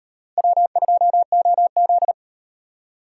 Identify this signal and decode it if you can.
Morse code, heard as a keyed tone, about 37 wpm, 700 Hz. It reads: W2OZ